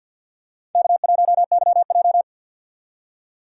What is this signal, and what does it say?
Morse code, heard as a keyed tone, about 52 wpm, 700 Hz. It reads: K1YJ
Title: USA Call Signs - 4 Characters - 3x Character Spacing - 50wpm